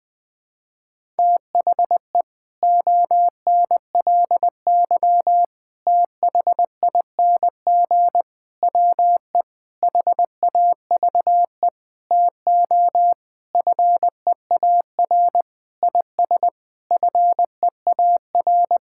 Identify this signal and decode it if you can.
Morse code, heard as a keyed tone, about 21 wpm, 700 Hz. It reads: THE ONLY THING WE HAVE TO FEAR IS FEAR
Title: Sentences with 9-Word Phrases - Spoken Beginning - 20wpm